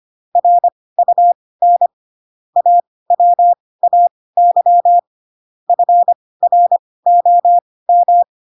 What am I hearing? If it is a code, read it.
Morse code, heard as a keyed tone, about 25 wpm, spaced as for 18 wpm, 700 Hz. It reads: RUN AWAY FROM
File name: Pinocchio / Pinocchio ch4 12 25 ik1hge